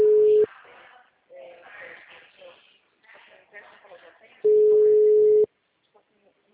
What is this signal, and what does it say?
Ringback tone (Europe, 425 Hz), with speech.
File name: out- -2048- - - .71004